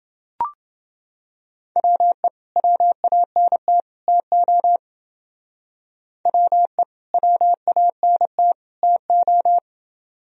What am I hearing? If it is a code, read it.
Morse code, heard as a keyed tone, about 28 wpm, 700 Hz. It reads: WE WANT TO WE WANT TO